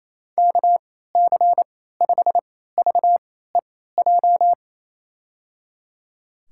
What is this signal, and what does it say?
Morse code, heard as a keyed tone, about 28 wpm, spaced as for 16 wpm, 700 Hz. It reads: KC5VEJ
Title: USA Call Signs - 6 Characters - 3x Character Spacing - 28wpm